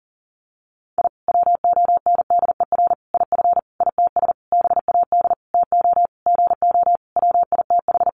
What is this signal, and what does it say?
Morse code, heard as a keyed tone, about 40 wpm, 700 Hz. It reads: I WONDER IF ITS BAD TO GO WITH
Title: Sentences from Top 700 Words 40wpm